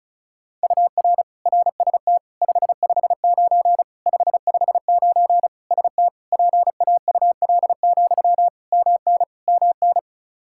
Morse code, heard as a keyed tone, about 36 wpm, 700 Hz. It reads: UR RST 559 559 ST PAUL, MN MN
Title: Practice QSO - Exchange 1 - Beginning - 35wpm